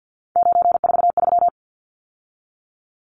Morse code, heard as a keyed tone, about 48 wpm, 700 Hz. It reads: 943
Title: Numbers - 3-Digit 50wpm